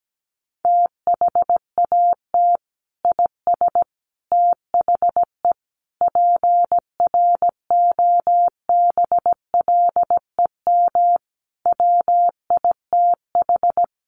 Morse code, heard as a keyed tone, about 17 wpm, 700 Hz. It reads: THAT IS THE PROBLEM WITH
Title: Sentences from Top 1100 Words 17wpm